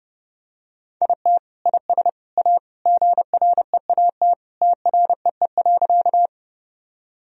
Morse code, heard as a keyed tone, about 30 wpm, 700 Hz. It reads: IT IS A GREAT TREE.
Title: Sentences from Top 1000 Words - Review Part 1 - 30wpm